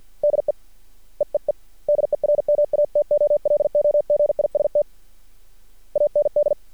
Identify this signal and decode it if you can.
Morse code, heard as a keyed tone, about 50 wpm, 590 Hz. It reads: DE EEE DERMATOLOGIST AND
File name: archived / CRSnet